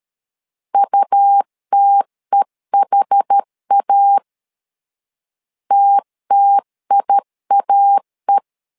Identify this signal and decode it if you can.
Morse code, heard as a keyed tone, about 12 wpm, 800 Hz. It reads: UTEHA TTIAE
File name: LEZIONI / 2 UHDN / 2 UH 12wpm 1439